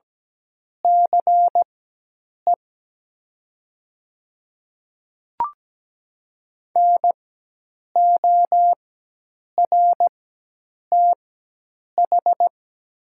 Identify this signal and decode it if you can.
Morse code, heard as a keyed tone, about 17 wpm, spaced as for 8 wpm, 700 Hz. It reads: CE NORTH
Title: 4x Character Spacing - Top 500 Words - 17wpm